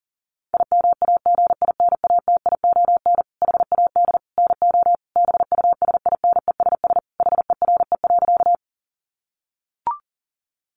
Morse code, heard as a keyed tone, about 40 wpm, 700 Hz. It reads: IMAGINATION HAD NO BUSINESS HERE.
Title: Sentences from Top 1300 Words 40wpm